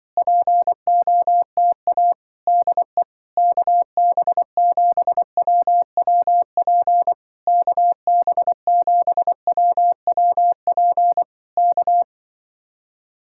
Morse code, heard as a keyed tone, about 24 wpm, 700 Hz. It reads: POTA DE KB7WWP KB7WWP K
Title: Speed-Racing Common QSO Phrases Encore - 20wpm